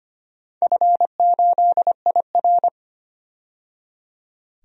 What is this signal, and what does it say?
Morse code, heard as a keyed tone, about 25 wpm, 700 Hz. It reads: F8IR
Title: DXCC-100 Call Signs - 4 Characters - 25wpm